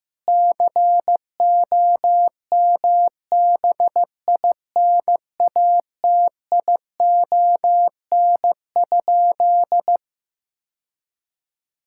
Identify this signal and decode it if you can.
Morse code, heard as a keyed tone, about 16 wpm, 700 Hz. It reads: COMBINATION?
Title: Sentences from Top 1000 Words - Review Part 6 - 15wpm